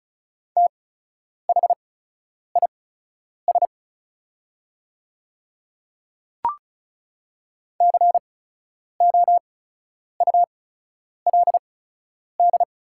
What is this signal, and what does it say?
Morse code, heard as a keyed tone, about 36 wpm, spaced as for 10 wpm, 700 Hz. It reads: THIS COULD
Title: 8x Character Spacing - Top 100 Words - 35wpm